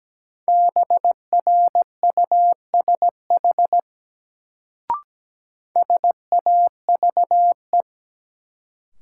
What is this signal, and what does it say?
Morse code, heard as a keyed tone, about 17 wpm, 700 Hz. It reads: BRUSH SAVE
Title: Sentences from Top 1100 Words - New Word Drill - 17wpm